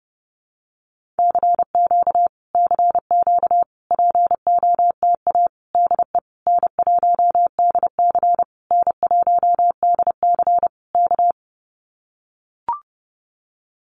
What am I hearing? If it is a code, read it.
Morse code, heard as a keyed tone, about 30 wpm, 700 Hz. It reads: CQ CQ POTA DE N1DC N1DC K
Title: Common QSO Phrases Encore 30wpm